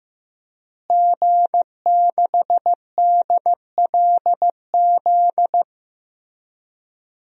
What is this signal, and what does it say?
Morse code, heard as a keyed tone, about 15 wpm, 700 Hz. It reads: G6DLZ
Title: DXCC-100 Call Signs - 5 Characters - Repeated 3x - 15wpm